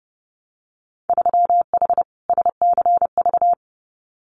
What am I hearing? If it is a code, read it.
Morse code, heard as a keyed tone, about 31 wpm, 700 Hz. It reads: 3H SCV
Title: Winter Field Day Exchanges Encore 30wpm